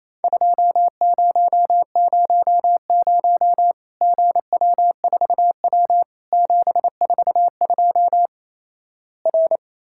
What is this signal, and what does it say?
Morse code, heard as a keyed tone, about 28 wpm, 700 Hz. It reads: 2000 GW4W 742 R
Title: Worked All Europe DX Contest - QTC Exchange - 28wpm